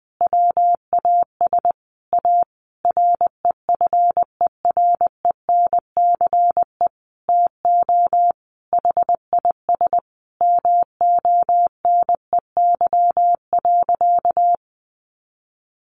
Morse code, heard as a keyed tone, about 20 wpm, 700 Hz. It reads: WAS A REFERENCE TO HIS MONEY.
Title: Sentences from Top 1200 Words 20wpm